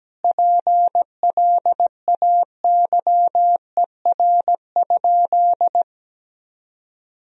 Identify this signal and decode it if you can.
Morse code, heard as a keyed tone, about 17 wpm, 700 Hz. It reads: PLAYER?
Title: Sentences from Top 1000 Words - Review Part 4 - 17wpm